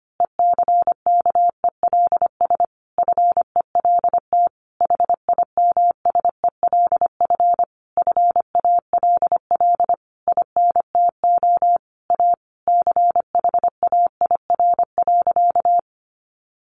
Morse code, heard as a keyed tone, about 25 wpm, 700 Hz. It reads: ECKELS FELT HIMSELF FALL INTO A CHAIR.
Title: A Sound Of Thunder 25wpm